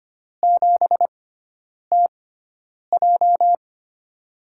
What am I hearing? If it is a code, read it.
Morse code, heard as a keyed tone, about 25 wpm, spaced as for 9 wpm, 700 Hz. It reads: 7TJ
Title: USA Call Signs - 6x Character Spacing - 25wpm